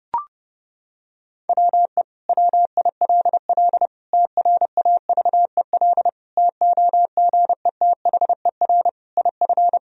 Morse code, heard as a keyed tone, about 32 wpm, 700 Hz. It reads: WE WILL TRAVEL TOGETHER IF
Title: Sentences from Top 1000 Words - Review Part 2 - 30wpm